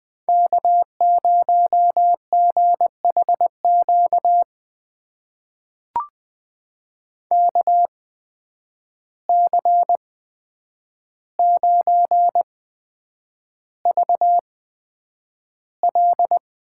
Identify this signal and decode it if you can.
Morse code, heard as a keyed tone, about 20 wpm, spaced as for 9 wpm, 700 Hz. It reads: K0GHQ KC9VL
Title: USA Call Signs - 5 Characters - 8x Character Spacing - 20wpm